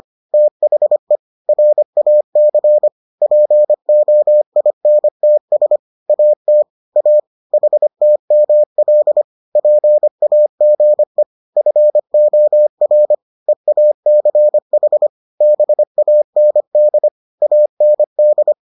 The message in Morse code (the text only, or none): THE RAC POINTS AT A HTML PAGE FOR EACH BAND AND